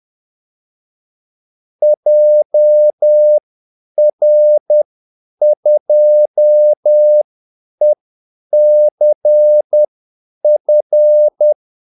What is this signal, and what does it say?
Morse code, heard as a keyed tone, about 10 wpm, spaced as for 8 wpm, 600 Hz. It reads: JR2ECF
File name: Callsigns-10-8wpm